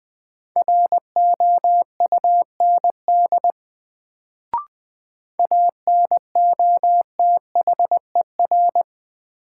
Morse code, heard as a keyed tone, about 20 wpm, 700 Hz. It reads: ROUND ANOTHER
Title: Sets of 5 Words Encore - New Word Drill - 20wpm